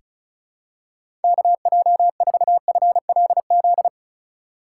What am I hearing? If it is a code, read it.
Morse code, heard as a keyed tone, about 36 wpm, 700 Hz. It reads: KJ4FLZ
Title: USA Call Signs - 6 Characters - 4x Character Spacing - 35wpm